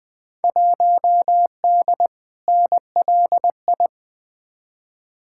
Morse code, heard as a keyed tone, about 20 wpm, 700 Hz. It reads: 1D NLI
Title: Field Day Exchanges - 20wpm